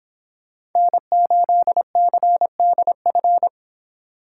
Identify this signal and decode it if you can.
Morse code, heard as a keyed tone, about 26 wpm, 700 Hz. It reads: N8CDF